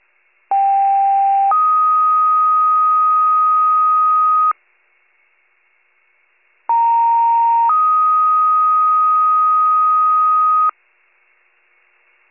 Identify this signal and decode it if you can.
Two-tone page, heard as a one-second tone, then a three-second tone, about 790 then 1250 Hz, about 930 then 1250 Hz. It is heard from 0.5 to 10.7 s.